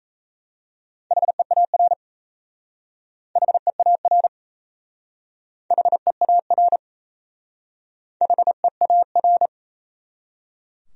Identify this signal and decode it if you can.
Morse code, heard as a keyed tone, about 35 wpm, 700 Hz. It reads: HEAR HEAR HEAR HEAR